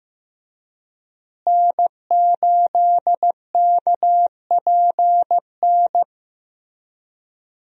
Morse code, heard as a keyed tone, about 16 wpm, 700 Hz. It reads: N8KPN
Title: USA Call Signs - 5 Characters - Repeated 3x - 15wpm